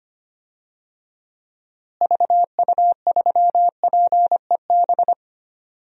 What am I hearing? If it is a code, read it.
Morse code, heard as a keyed tone, about 25 wpm, 700 Hz. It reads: VU3PEB
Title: DXCC-100 Call Signs - 6 Characters - Repeated 3x - 25wpm